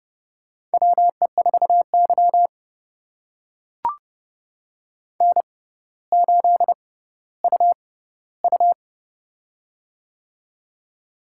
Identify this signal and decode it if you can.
Morse code, heard as a keyed tone, about 30 wpm, spaced as for 15 wpm, 700 Hz. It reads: WE4Y N8UU
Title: USA Call Signs - 4 Characters - 6x Character Spacing - 30wpm